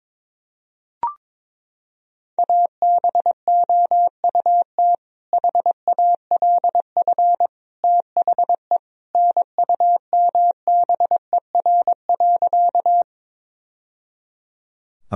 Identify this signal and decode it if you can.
Morse code, heard as a keyed tone, about 22 wpm, 700 Hz. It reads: ABOUT HALF THE NUMBER.
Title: Sentences from Top 1000 Words - Review Part 1 - 22wpm